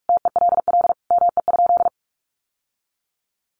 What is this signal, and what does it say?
Morse code, heard as a keyed tone, about 44 wpm, 700 Hz. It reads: TELL ME?
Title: Sentences from Top 600 Words 45wpm